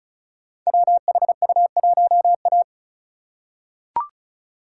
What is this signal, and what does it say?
Morse code, heard as a keyed tone, about 36 wpm, 690 Hz. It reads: WHU1A
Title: Code Groups - 5 Letter:Numbers - 35wpm